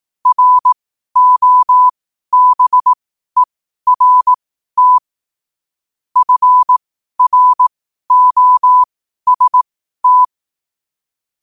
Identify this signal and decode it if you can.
Morse code, heard as a keyed tone, about 18 wpm, spaced as for 12 wpm, 1000 Hz. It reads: ROBERT FROST
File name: qotdfiles / 12wpm